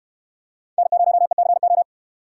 Morse code, heard as a keyed tone, about 66 wpm, 700 Hz. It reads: N0ECO